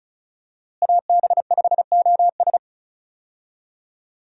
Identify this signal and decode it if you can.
Morse code, heard as a keyed tone, about 34 wpm, 700 Hz. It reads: AB5OS
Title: USA Call Signs - 5 Characters - Repeated 2x - 35wpm